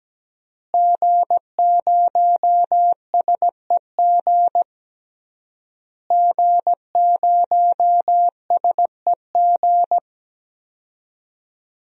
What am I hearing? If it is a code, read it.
Morse code, heard as a keyed tone, about 17 wpm, 700 Hz. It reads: G0SEG G0SEG
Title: DXCC-100 Call Signs - 5 Characters - Repeated 3x - 17wpm